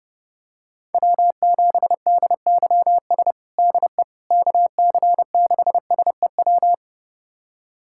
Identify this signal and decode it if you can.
Morse code, heard as a keyed tone, about 30 wpm, 690 Hz. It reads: W7DYS DE KC6SEW
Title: USA Callsign DE Callsign 30wpm